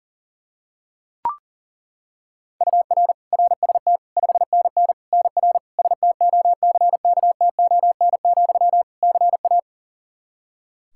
Morse code, heard as a keyed tone, about 40 wpm, 700 Hz. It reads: UR RST 5NN NR STOCKTON, CA
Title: Practice QSO - Exchange 1 - Beginning - 40wpm